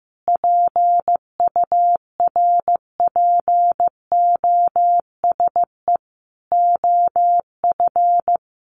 Morse code, heard as a keyed tone, about 15 wpm, 700 Hz. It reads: PURPOSE OF
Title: Sentences from Top 1400 Words 15wpm